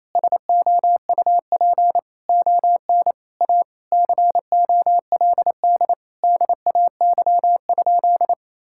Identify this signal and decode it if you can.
Morse code, heard as a keyed tone, about 27 wpm, 700 Hz. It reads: SOUP ON A COLD DAY?
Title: Sentences from Top 1000 Words - Review Part 6 - 28wpm